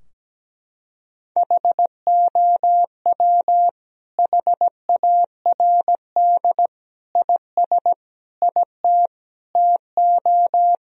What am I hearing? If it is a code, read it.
Morse code, heard as a keyed tone, about 17 wpm, 700 Hz. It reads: HOW HARD IS IT TO
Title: Sentences from Top 1000 Words - Review Part 3 - 17wpm